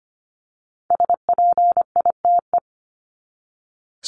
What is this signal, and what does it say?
Morse code, heard as a keyed tone, about 25 wpm, 700 Hz. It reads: SPITE
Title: Top 1751-2000 Words 25wpm